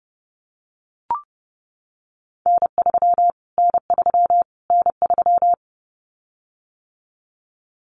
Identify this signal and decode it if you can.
Morse code, heard as a keyed tone, about 30 wpm, 700 Hz. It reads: N3 N3 N3
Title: Callsign Prefixes for Top 25 DXCC - Repeated 3x - 30wpm